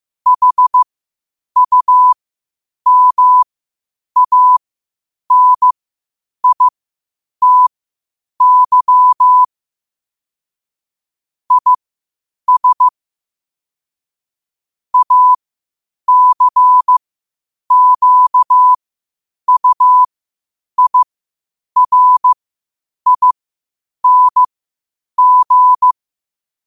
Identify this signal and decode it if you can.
Morse code, heard as a keyed tone, about 15 wpm, spaced as for 8 wpm, 1000 Hz. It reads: HUMANITY IS ACQUIRING
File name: 8wpm